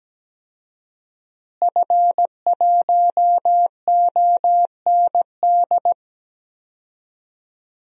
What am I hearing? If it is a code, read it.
Morse code, heard as a keyed tone, about 17 wpm, 700 Hz. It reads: F1OND
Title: DXCC-100 Call Signs - 5 Characters - Repeated 2x - 17wpm